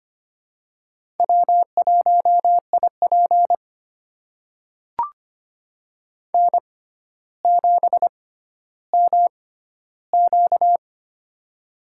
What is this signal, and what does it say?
Morse code, heard as a keyed tone, about 26 wpm, spaced as for 13 wpm, 700 Hz. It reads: W1IP N7MQ
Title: USA Call Signs - 4 Characters - 6x Character Spacing - 25wpm